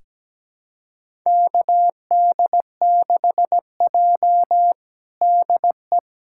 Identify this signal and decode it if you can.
Morse code, heard as a keyed tone, about 17 wpm, 700 Hz. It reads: KD6J DE
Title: USA Callsign DE Callsign - 4 Characters - 17wpm